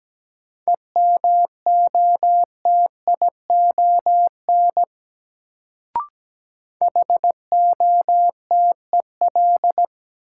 Morse code, heard as a keyed tone, about 17 wpm, 700 Hz. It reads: EMOTION HOTEL
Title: Sentences from Top 1600 Words - New Word Drill - 17wpm